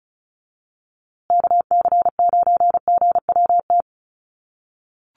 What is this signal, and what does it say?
Morse code, heard as a keyed tone, about 34 wpm, 700 Hz. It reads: KC9GWT
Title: USA Call Signs Repeated 3x 35wpm